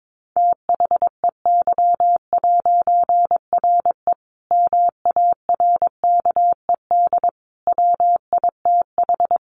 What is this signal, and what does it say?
Morse code, heard as a keyed tone, about 22 wpm, 700 Hz. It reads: THEY'RE MARKED WITH
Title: A Sound Of Thunder 22wpm